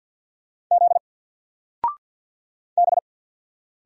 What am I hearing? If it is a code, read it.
Morse code, heard as a keyed tone, about 48 wpm, 700 Hz. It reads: Z B
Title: Single Letter-Number Alt 50wpm